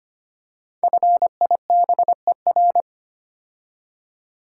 Morse code, heard as a keyed tone, about 25 wpm, 700 Hz. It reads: FIBER
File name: 5 Letter Words - First 3 Letters Given - 25wpm